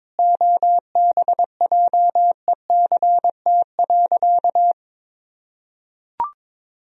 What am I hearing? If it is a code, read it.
Morse code, heard as a keyed tone, about 22 wpm, 700 Hz. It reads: OBJECT.
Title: Sentences from Top 1000 Words - Review Part 3 - 22wpm